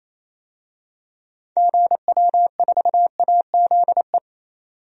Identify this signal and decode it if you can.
Morse code, heard as a keyed tone, about 28 wpm, 700 Hz. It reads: GW4AZE
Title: DXCC-100 Call Signs - 6 Characters - 28wpm